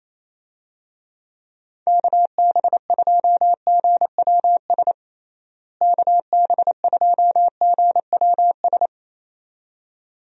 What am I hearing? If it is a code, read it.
Morse code, heard as a keyed tone, about 28 wpm, 700 Hz. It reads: KB2GWS KB2GWS
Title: USA Call Signs - 6 Characters - Repeated 2x - 28wpm